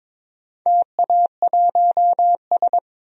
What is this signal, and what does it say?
Morse code, heard as a keyed tone, about 22 wpm, 700 Hz. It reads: TA1S
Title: DXCC-100 Call Signs - 4 Characters - 22wpm